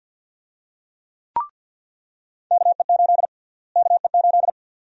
Morse code, heard as a keyed tone, about 49 wpm, 700 Hz. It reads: KE8 KE8
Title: USA Call Sign Prefixes - Repeated 2x - 50wpm